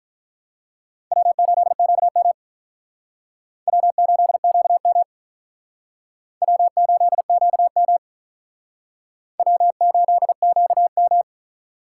Morse code, heard as a keyed tone, about 44 wpm, 700 Hz. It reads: W8QM W8QM W8QM W8QM
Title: Speed Racing - USA 1x2 Call Signs 35wpm